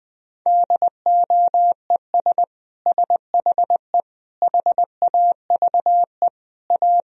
Morse code, heard as a keyed tone, about 20 wpm, 700 Hz. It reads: DOES SHE HAVE A
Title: Sentences from Top 1000 Words - Review Part 2 - 20wpm